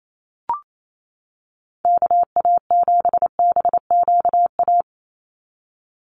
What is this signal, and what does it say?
Morse code, heard as a keyed tone, about 28 wpm, 700 Hz. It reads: KA7BQA
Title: USA Call Signs 28wpm